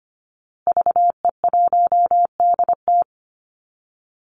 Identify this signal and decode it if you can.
Morse code, heard as a keyed tone, about 25 wpm, 700 Hz. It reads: VE1DT
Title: Call Signs Repeated 3x - 25wpm